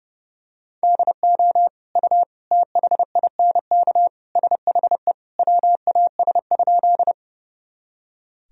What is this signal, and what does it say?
Morse code, heard as a keyed tone, about 30 wpm, 700 Hz. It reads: DO U THINK SHE WAS?
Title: Sentences from Top 200 Words and Abbreviations 30wpm